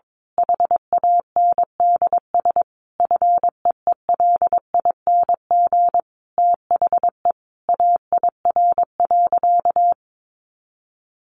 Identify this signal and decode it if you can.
Morse code, heard as a keyed tone, about 22 wpm, 700 Hz. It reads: HANDS FEELING THE AIR.
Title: A Sound Of Thunder 22wpm